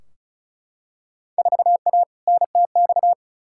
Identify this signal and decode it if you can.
Morse code, heard as a keyed tone, about 36 wpm, 700 Hz. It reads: 4A NTX
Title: Field Day Exchanges - 35wpm